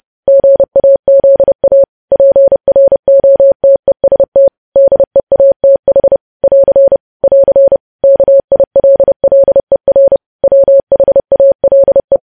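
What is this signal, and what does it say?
Morse code, heard as a keyed tone, about 30 wpm, 550 Hz. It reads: GAZA PROTEST DEATH <AR> <AR> KILLER WHALE
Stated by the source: Morse code practice of news headlines at 30.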